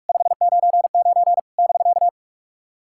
Morse code, heard as a keyed tone, about 48 wpm, 700 Hz. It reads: 599 <BK>
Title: SOTA QSO Practice 45wpm